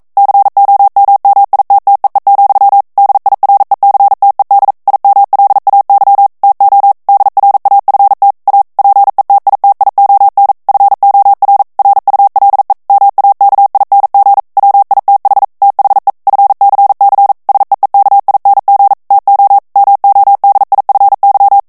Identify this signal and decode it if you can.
Morse code, heard as a keyed tone, about 42 wpm, 800 Hz. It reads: COMMITTEE, DIRECTED IMLAY TO DRAFT A PETITION FOR RULE MAKING WITH THE FCC SEEKING TO MODIFY